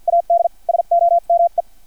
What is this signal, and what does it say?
Morse code, heard as a keyed tone, about 50 wpm, spaced as for 43 wpm, 680 Hz. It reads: AD SOME
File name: archived / CRSnet